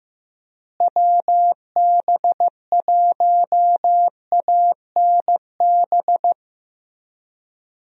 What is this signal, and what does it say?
Morse code, heard as a keyed tone, about 15 wpm, 700 Hz. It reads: WB1ANB
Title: USA Call Signs - 6 Characters - Repeated 3x - 15wpm